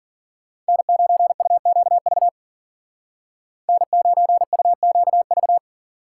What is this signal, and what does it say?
Morse code, heard as a keyed tone, about 52 wpm, 700 Hz. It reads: N9UQV N9UQV